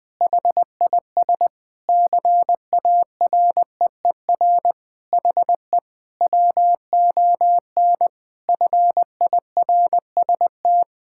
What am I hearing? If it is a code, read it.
Morse code, heard as a keyed tone, about 20 wpm, 700 Hz. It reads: HIS CAREER HE WON FIRST
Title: Sentences with 7-Word Phrases - Spoken Beginning - 20wpm